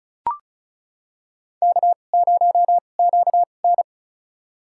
Morse code, heard as a keyed tone, about 34 wpm, spaced as for 25 wpm, 700 Hz. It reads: K0QN
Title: USA Call Signs - 2x Character Spacing - 35wpm